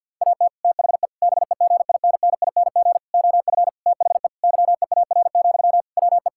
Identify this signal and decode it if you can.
Morse code, heard as a keyed tone, about 50 wpm, 700 Hz. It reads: AT THE BEGINNING OF THE YEAR, WE
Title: Sentences from Top 1000 Words - Review Part 5 - 50wpm